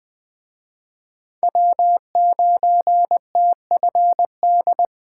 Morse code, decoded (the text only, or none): W9TFD